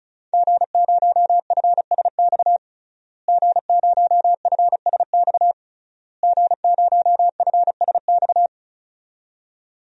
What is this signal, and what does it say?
Morse code, heard as a keyed tone, about 36 wpm, 700 Hz. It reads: G0FSX G0FSX G0FSX
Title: DXCC-100 Call Signs - 5 Characters - Repeated 3x - 35wpm